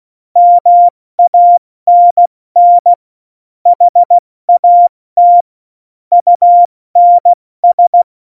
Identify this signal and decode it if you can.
Morse code, heard as a keyed tone, about 16 wpm, 700 Hz. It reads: MANN HAT UNS
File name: Morsekurs / Witze / Witz 08. September 2020, 70 F80